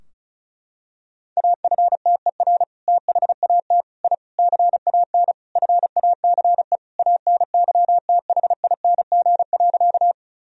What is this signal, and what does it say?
Morse code, heard as a keyed tone, about 36 wpm, 700 Hz. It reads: AFTER THAT I CAN FACE ANYTHING.
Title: Sentences from Top 1000 Words - Review Part 2 - 35wpm